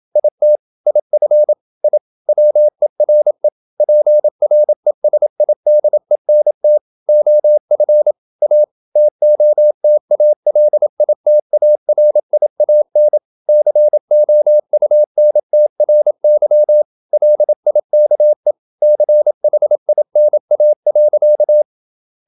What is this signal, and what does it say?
Morse code, heard as a keyed tone, about 27 wpm, 600 Hz. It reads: IT IF I WERE PRESIDENT OF A TOTALITARIAN COUNTRY LIKE CHINA.
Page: Never 26wpm | CW med Gnister